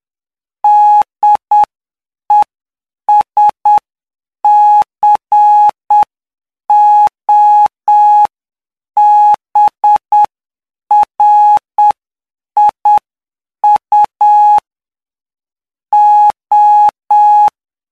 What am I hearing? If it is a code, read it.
Morse code, heard as a keyed tone, about 8 wpm, spaced as for 7 wpm, 840 Hz. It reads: DESCOBRIU O
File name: cw / py2lcd / 14-PY2LCD-Li��o 23